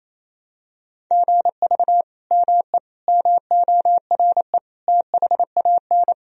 Morse code, heard as a keyed tone, about 28 wpm, 700 Hz. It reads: GV ME MORE THAN
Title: Sentences from Top 200 Words and Abbreviations 28wpm